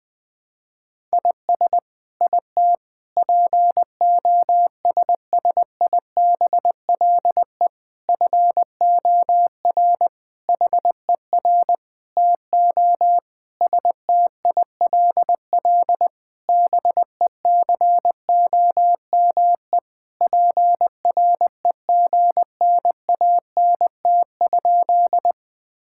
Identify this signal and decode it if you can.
Morse code, heard as a keyed tone, about 20 wpm, 700 Hz. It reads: IS IT POSSIBLE FOR HER TO STILL BECOME PREGNANT?
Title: Sentences from Top 1000 Words - Review Part 6 - 20wpm